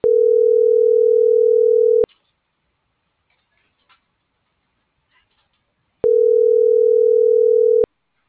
Ringback tone (North America), 440 + 480 Hz.